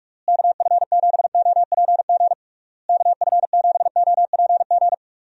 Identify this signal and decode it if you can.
Morse code, heard as a keyed tone, about 41 wpm, 700 Hz. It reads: KF7OPG KF7OPG
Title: USA Call Signs - 6 Characters - Repeated 3x - 45wpm